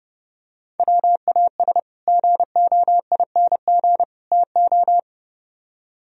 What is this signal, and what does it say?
Morse code, heard as a keyed tone, about 30 wpm, 700 Hz. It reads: WAS GOING TO